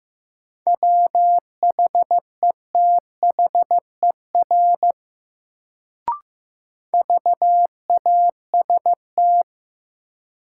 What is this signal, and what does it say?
Morse code, heard as a keyed tone, about 15 wpm, 700 Hz. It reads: WHETHER VAST
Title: Sentences from Top 1300 Words - New Word Drill Alt - 15wpm